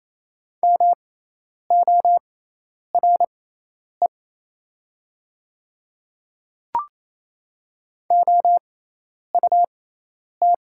Morse code, heard as a keyed tone, about 28 wpm, spaced as for 10 wpm, 700 Hz. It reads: MORE OUT
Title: 6x Character Spacing - Top 100 Words - 28wpm